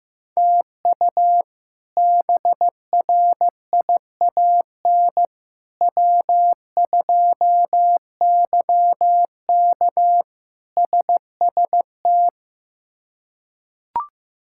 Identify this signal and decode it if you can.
Morse code, heard as a keyed tone, about 15 wpm, 700 Hz. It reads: TU BRIAN W2YK SST
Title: K1USN SST QSO Practice - 15wpm